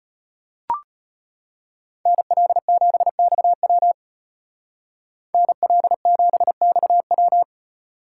Morse code, heard as a keyed tone, about 33 wpm, 700 Hz. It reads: NL7XW NL7XW